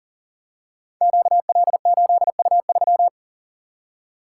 Morse code, heard as a keyed tone, about 42 wpm, 700 Hz. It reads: QL8U3
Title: Code Groups - 5 Letter:Numbers - 40wpm